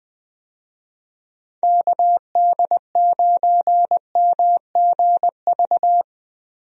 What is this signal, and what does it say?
Morse code, heard as a keyed tone, about 20 wpm, 700 Hz. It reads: KD9MGV
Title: USA Call Signs - 6 Characters - Repeated 2x - 20wpm